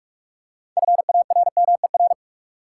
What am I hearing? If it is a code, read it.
Morse code, heard as a keyed tone, about 46 wpm, 700 Hz. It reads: FARMER
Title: Six Letter Words - 45wpm